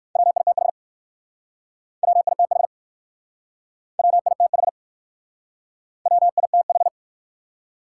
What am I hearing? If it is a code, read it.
Morse code, heard as a keyed tone, about 56 wpm, 700 Hz. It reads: WITH WITH WITH WITH